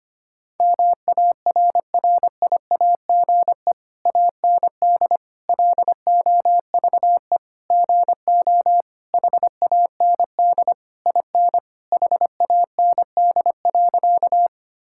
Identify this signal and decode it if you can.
Morse code, heard as a keyed tone, about 25 wpm, 700 Hz. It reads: MARRIAGE AND LOVE GO HAND IN HAND.
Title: Sentences from Top 1000 Words - Review Part 5 - 25wpm